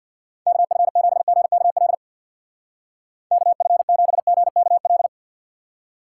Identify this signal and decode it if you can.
Morse code, heard as a keyed tone, about 63 wpm, 700 Hz. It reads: KF7GKL KF7GKL